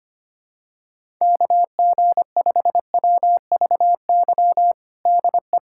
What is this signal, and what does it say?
Morse code, heard as a keyed tone, about 25 wpm, 700 Hz. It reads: KG5WVY DE
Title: USA Callsign DE Callsign - 6 Characters - 25wpm